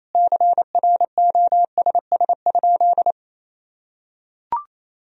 Morse code, heard as a keyed tone, about 28 wpm, 700 Hz. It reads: CROSS?
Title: Sentences from Top 300 Words and Abbreviations 28wpm